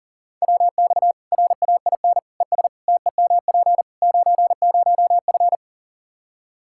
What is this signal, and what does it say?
Morse code, heard as a keyed tone, about 40 wpm, 700 Hz. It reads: WX RAIN ES TEMP 90F
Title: Practice QSO - Exchange 2 - Info - 40wpm